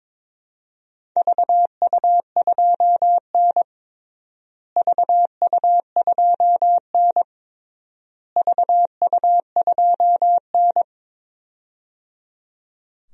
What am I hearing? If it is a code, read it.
Morse code, heard as a keyed tone, about 22 wpm, 700 Hz. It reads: VU2N VU2N VU2N
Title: DXCC-100 Call Signs - 4 Characters - Repeated 3x - 22wpm